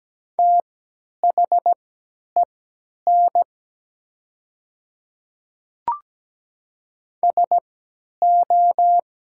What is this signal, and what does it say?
Morse code, heard as a keyed tone, about 17 wpm, spaced as for 10 wpm, 700 Hz. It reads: THEN SO
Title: 3x Character Spacing - Top 100 Words - 17wpm